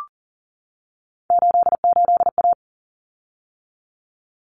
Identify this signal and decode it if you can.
Morse code, heard as a keyed tone, about 40 wpm, 700 Hz. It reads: 88A
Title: Common Cut Numbers - 3-Digit 40wpm